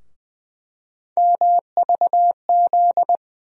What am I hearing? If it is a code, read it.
Morse code, heard as a keyed tone, about 20 wpm, 700 Hz. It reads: MVZ
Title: USA Call Sign Postfixes - 3 Letters 20wpm